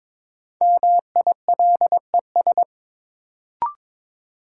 Morse code, heard as a keyed tone, about 22 wpm, 700 Hz. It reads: MILES
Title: Sentences from Top 1000 Words - Review Part 3 - New Word Drill - 22wpm